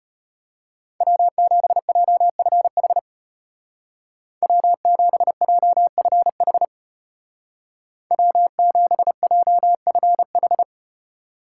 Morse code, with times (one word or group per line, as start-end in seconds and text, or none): 1.0-3.0 W7JFH
4.4-6.6 W7JFH
8.1-10.6 W7JFH